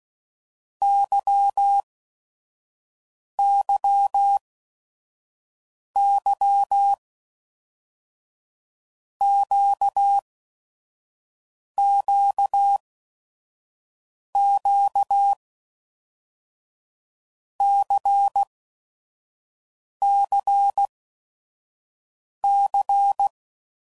Morse code, heard as a keyed tone, about 16 wpm, 800 Hz. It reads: Y Y Y Q Q Q C C C
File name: Basic Letter Groups By7Groups / LFCYQ / LFCYQ@5 f15